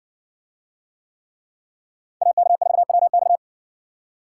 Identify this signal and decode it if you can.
Morse code, heard as a keyed tone, about 56 wpm, 700 Hz. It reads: AK4WX